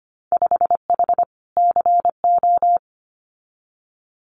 Morse code, heard as a keyed tone, about 25 wpm, 690 Hz. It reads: 5H CO
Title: Winter Field Day Exchanges Encore 25wpm